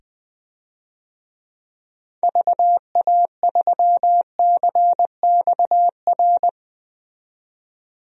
Morse code, heard as a keyed tone, about 20 wpm, 700 Hz. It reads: VA3CXR
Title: DXCC-100 Call Signs - 6 Characters - 20wpm